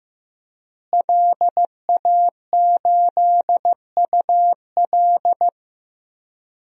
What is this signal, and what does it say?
Morse code, heard as a keyed tone, about 15 wpm, 700 Hz. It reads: LA8UL
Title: DXCC-100 Call Signs - 5 Characters - Repeated 2x - 15wpm